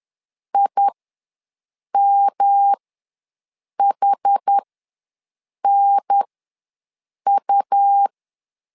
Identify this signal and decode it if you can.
Morse code, heard as a keyed tone, about 10 wpm, spaced as for 6 wpm, 800 Hz. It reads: IMHNU
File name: LEZIONI / 3 MWRO / 3 M ws 10wpm 2121